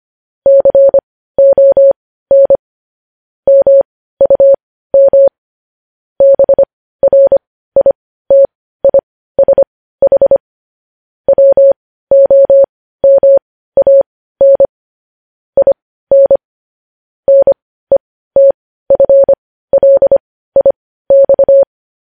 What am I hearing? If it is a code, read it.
Morse code, heard as a keyed tone, about 25 wpm, spaced as for 15 wpm, 550 Hz. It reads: CON MUM BRITISH WOMAN IN NETFLIX
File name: podcast / News headlines-1x-25x15